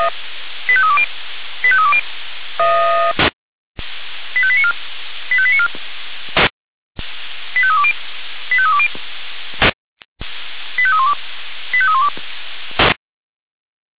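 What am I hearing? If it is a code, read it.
Five-tone selective call (ZVEI), heard as quick runs of five tones, tones of about 70 ms. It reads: 85320 85320 85894 85894 85320 85320 85312 85312
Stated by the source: ZVEI five-tone selective call received on an SDR.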